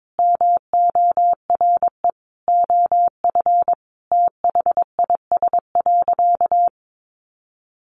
Morse code, heard as a keyed tone, about 22 wpm, 700 Hz. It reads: MORE OF THIS.
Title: Sentences from Top 1100 Words 22wpm